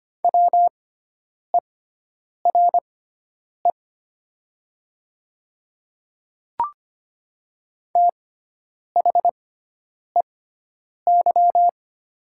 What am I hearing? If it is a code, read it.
Morse code, heard as a keyed tone, about 25 wpm, spaced as for 9 wpm, 700 Hz. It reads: WERE THEY
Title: 6x Character Spacing - Top 100 Words - 25wpm